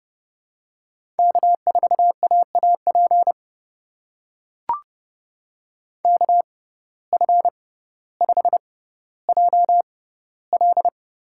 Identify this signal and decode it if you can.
Morse code, heard as a keyed tone, about 31 wpm, spaced as for 15 wpm, 700 Hz. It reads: K4AAP KF5JL
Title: USA Call Signs - 5 Characters - 6x Character Spacing - 30wpm